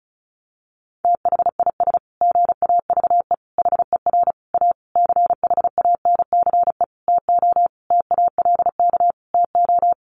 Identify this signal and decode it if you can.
Morse code, heard as a keyed tone, about 34 wpm, 700 Hz. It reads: THIS GAVE HER A CHANCE TO TALK TO
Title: Sentences from Top 1400 Words 35wpm